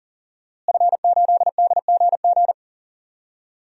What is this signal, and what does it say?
Morse code, heard as a keyed tone, about 40 wpm, 700 Hz. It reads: F8DGG
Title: DXCC-100 Call Signs - 5 Characters - Repeated 3x - 40wpm